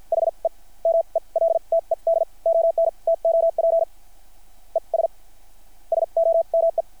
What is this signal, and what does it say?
Morse code, heard as a keyed tone, about 50 wpm, spaced as for 39 wpm, 660 Hz. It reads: HE MELTED ON TOP ES SOME
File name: archived / CRSnet